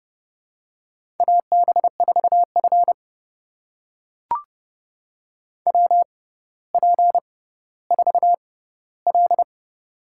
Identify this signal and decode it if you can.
Morse code, heard as a keyed tone, about 30 wpm, spaced as for 15 wpm, 700 Hz. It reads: AB4F WP4L
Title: USA Call Signs - 4 Characters - 6x Character Spacing - 30wpm